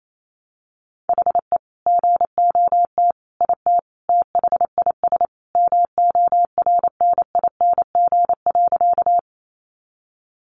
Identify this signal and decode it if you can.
Morse code, heard as a keyed tone, about 28 wpm, 700 Hz. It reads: HE GOT IT THIS MORNING.
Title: Sentences from Top 400 Words 28wpm